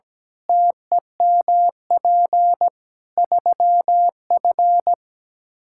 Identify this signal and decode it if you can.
Morse code, heard as a keyed tone, about 17 wpm, 700 Hz. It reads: TEMP 3F
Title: K1USN SST QSO Practice 17wpm